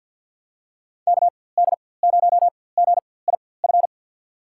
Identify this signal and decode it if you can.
Morse code, heard as a keyed tone, about 47 wpm, spaced as for 23 wpm, 700 Hz. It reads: KD0GIF
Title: USA Call Signs - 6 Characters - 4x Character Spacing - 50wpm